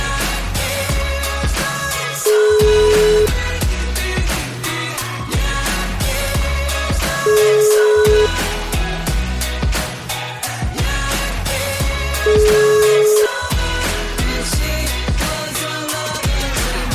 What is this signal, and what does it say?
Ringback tone (Europe, 425 Hz), with noise.